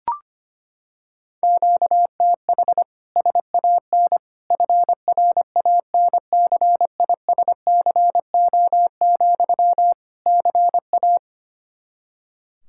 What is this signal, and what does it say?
Morse code, heard as a keyed tone, about 25 wpm, 700 Hz. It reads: QTH SAN FRANCISCO, CA
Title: Practice QSO - Exchange 1 - Beginning - 25wpm